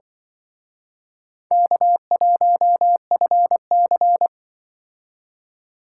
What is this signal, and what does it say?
Morse code, heard as a keyed tone, about 25 wpm, 700 Hz. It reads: K1FC